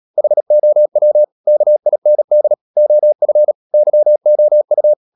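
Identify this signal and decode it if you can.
Morse code, heard as a keyed tone, about 37 wpm, 600 Hz. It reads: HOW KIND OF YOU
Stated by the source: Morse code practice at 35–39 wpm sending English text of Ken Follett's novel Never, from CW med Gnister.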